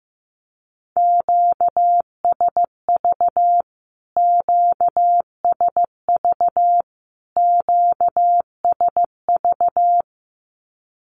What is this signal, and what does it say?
Morse code, heard as a keyed tone, about 15 wpm, 700 Hz. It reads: QSV QSV QSV
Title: Q-Codes Repeated 3x 15wpm